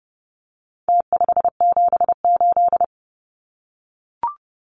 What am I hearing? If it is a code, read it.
Morse code, heard as a keyed tone, about 30 wpm, 700 Hz. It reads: T578
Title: Numbers - Contest Style - 4-Digit 30wpm